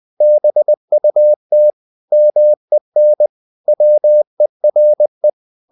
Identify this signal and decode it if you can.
Morse code, heard as a keyed tone, about 20 wpm, 600 Hz. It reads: BUT MEN WERE
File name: the old man and the sea / the old man and the sea part 1 20wpm